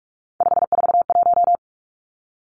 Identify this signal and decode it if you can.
Morse code, heard as a keyed tone, about 46 wpm, 700 Hz. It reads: 541
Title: Three Challenging Letter-Number Combinations 45wpm